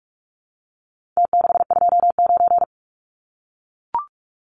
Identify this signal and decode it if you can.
Morse code, heard as a keyed tone, about 44 wpm, 700 Hz. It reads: T629
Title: Numbers - Contest Style - 4-Digit 45wpm